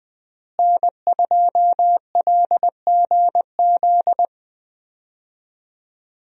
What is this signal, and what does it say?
Morse code, heard as a keyed tone, about 20 wpm, 700 Hz. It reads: N2LGZ
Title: USA Call Signs - 5 Characters - Repeated 3x - 20wpm